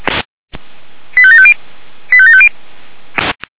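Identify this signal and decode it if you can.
Five-tone selective call (ZVEI), heard as quick runs of five tones, tones of about 70 ms. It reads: 85750 85750
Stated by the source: ZVEI five-tone call received by SDR, archived by the Feuerwehr Orlamünde fire brigade.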